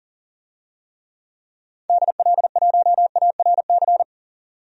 Morse code, heard as a keyed tone, about 37 wpm, 700 Hz. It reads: DL1ARC
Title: DXCC-100 Call Signs - 6 Characters - 40wpm